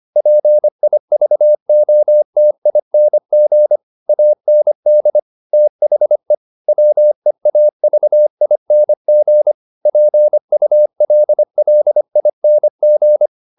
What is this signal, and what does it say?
Morse code, heard as a keyed tone, about 25 wpm, 600 Hz. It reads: PIVOTING AND THE WEAVING PULLING